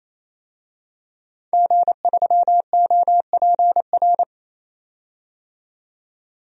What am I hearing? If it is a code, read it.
Morse code, heard as a keyed tone, about 28 wpm, 700 Hz. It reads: G3OPR
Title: DXCC-100 Call Signs - 5 Characters - 28wpm